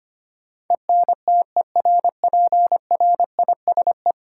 Morse code, heard as a keyed tone, about 25 wpm, 700 Hz. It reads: ENTERPRISE
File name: Ten Letter Words - 25wpm